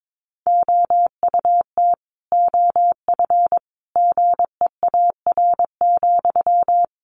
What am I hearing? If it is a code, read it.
Morse code, heard as a keyed tone, about 22 wpm, 700 Hz. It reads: OUT OF GEAR,
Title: Sentences from Top 1400 Words 22wpm